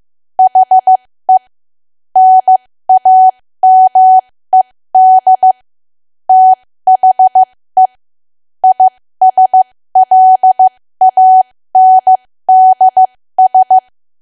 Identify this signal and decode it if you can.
Morse code, heard as a keyed tone, about 15 wpm, spaced as for 13 wpm, 750 Hz. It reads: HE NAMED THE ISLANDS
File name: cwa / W1AW-Level2 / 13WPM